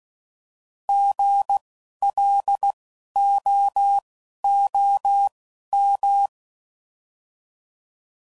Morse code, heard as a keyed tone, about 16 wpm, spaced as for 11 wpm, 800 Hz. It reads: GLOOM
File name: Prefixes / By First Letter